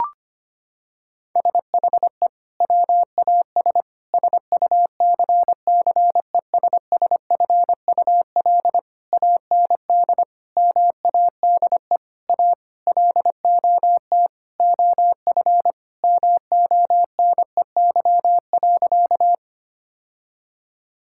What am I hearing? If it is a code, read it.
Morse code, heard as a keyed tone, about 25 wpm, 700 Hz. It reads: SHE WAS SUCCESSFUL AND MADE A LOT OF MONEY.
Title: Sentences from Top 1000 Words - Review Part 6 - 25wpm